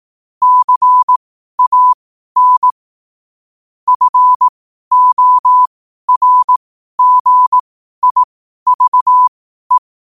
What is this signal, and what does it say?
Morse code, heard as a keyed tone, about 18 wpm, spaced as for 12 wpm, 1000 Hz. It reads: CAN FORGIVE
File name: qotdfiles / 12wpm